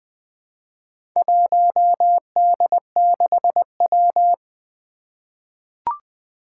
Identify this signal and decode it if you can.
Morse code, heard as a keyed tone, about 20 wpm, 700 Hz. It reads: 1D6W
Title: Four Challenging Letter-Number Combinations 20wpm